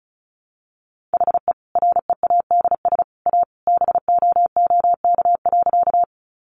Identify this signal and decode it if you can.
Morse code, heard as a keyed tone, about 34 wpm, 700 Hz. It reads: HE READS A BOOK.
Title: Additional Easy Phrases 35wpm